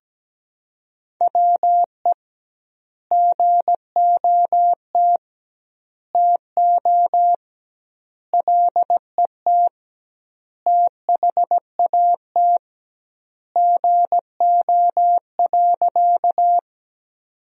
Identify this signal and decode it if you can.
Morse code, heard as a keyed tone, about 17 wpm, 700 Hz. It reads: WE GOT TO LET THAT GO.
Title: 2x Word-Spacing - Sentences from Top 300 Words - 17wpm